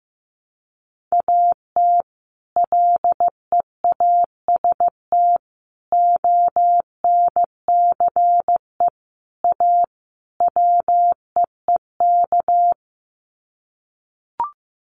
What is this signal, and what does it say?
Morse code, heard as a keyed tone, about 15 wpm, 700 Hz. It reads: AT LEAST ONCE A WEEK
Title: Sets of 5 Words Encore2 15wpm